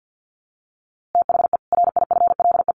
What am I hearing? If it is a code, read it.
Morse code, heard as a keyed tone, about 50 wpm, 700 Hz. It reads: THE RIFLE
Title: A Sound Of Thunder 50wpm